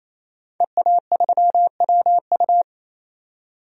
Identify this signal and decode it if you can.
Morse code, heard as a keyed tone, about 28 wpm, 700 Hz. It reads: EA3WU
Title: DXCC-100 Call Signs - 5 Characters - 28wpm